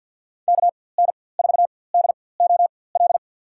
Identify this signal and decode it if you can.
Morse code, heard as a keyed tone, about 47 wpm, spaced as for 23 wpm, 700 Hz. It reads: KN4DOL
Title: USA Call Signs - 6 Characters - 4x Character Spacing - 50wpm